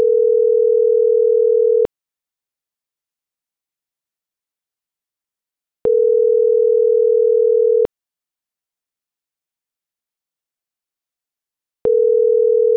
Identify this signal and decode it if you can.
Ringback tone (North America), 440 + 480 Hz.
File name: CENTRAL SAGOT / out adriana.poncec